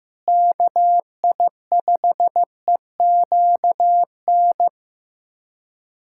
Morse code, heard as a keyed tone, about 16 wpm, 700 Hz. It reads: KI5EQN
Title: USA Callsign DE Callsign - 6 Characters - 15wpm